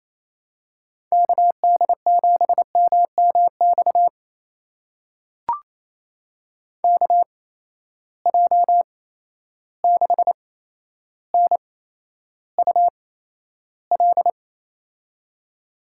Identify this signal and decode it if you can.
Morse code, heard as a keyed tone, about 28 wpm, spaced as for 12 wpm, 700 Hz. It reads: KD7MMX KJ6NUL
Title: USA Call Signs - 6 Characters - 8x Character Spacing - 28wpm